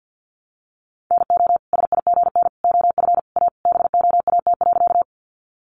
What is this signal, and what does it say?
Morse code, heard as a keyed tone, about 50 wpm, 700 Hz. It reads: NO SIGN OF A BOAT.
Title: Sentences from Top 1000 Words 50wpm